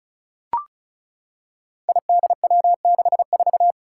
Morse code, heard as a keyed tone, about 36 wpm, 700 Hz. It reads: IDW64
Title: Five Challenging Letter-Number Combinations 35wpm